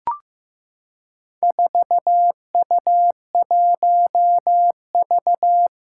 Morse code, heard as a keyed tone, about 15 wpm, 700 Hz. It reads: 4U1V
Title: Four Challenging Letter-Number Combinations 15wpm